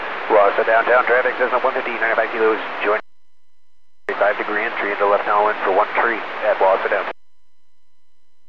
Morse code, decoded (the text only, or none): none